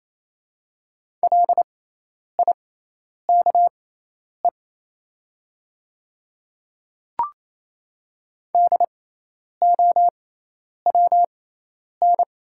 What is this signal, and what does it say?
Morse code, heard as a keyed tone, about 28 wpm, spaced as for 10 wpm, 700 Hz. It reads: LIKE DOWN
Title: 6x Character Spacing - Top 200 Words - 28wpm